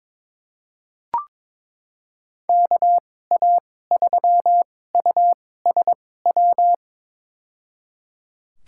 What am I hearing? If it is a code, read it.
Morse code, heard as a keyed tone, about 22 wpm, spaced as for 16 wpm, 700 Hz. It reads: KA3USW
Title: USA Call Signs - 6 Characters - 2x Character Spacing - 22wpm